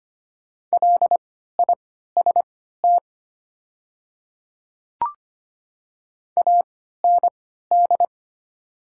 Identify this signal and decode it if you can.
Morse code, heard as a keyed tone, about 25 wpm, spaced as for 14 wpm, 700 Hz. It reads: LIST AND
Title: 3x Character Spacing - Top 500 Words - 25wpm